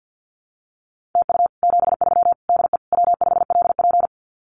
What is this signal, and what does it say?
Morse code, heard as a keyed tone, about 50 wpm, 700 Hz. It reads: TU 73 DE W5LP
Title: Common QSO Phrases Encore 50wpm